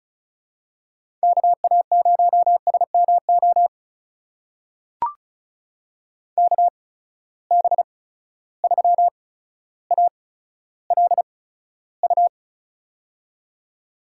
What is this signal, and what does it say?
Morse code, heard as a keyed tone, about 36 wpm, spaced as for 15 wpm, 700 Hz. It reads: KA0SMO KB3ALU
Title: USA Call Signs - 6 Characters - 8x Character Spacing - 35wpm